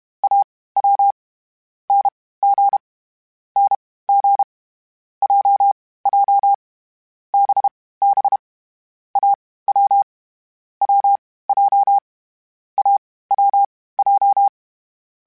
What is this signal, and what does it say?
Morse code, heard as a keyed tone, about 32 wpm, spaced as for 18 wpm, 800 Hz. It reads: AW NG NG JJ BB AW WJ AWJ
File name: Short Runs / Short Runs-mp3s / Unballanced @30